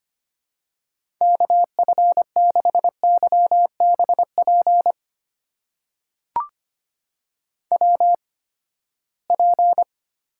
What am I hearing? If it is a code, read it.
Morse code, heard as a keyed tone, about 25 wpm, spaced as for 17 wpm, 700 Hz. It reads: KF6YBP WP
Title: USA Call Signs - 6 Characters - 8x Character Spacing - 25wpm